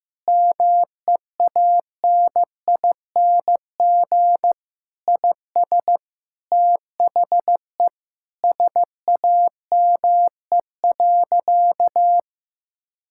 Morse code, heard as a keyed tone, about 15 wpm, 700 Hz. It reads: MEANING IS THE SAME.
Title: Sentences from Top 1000 Words - Review Part 6 - 15wpm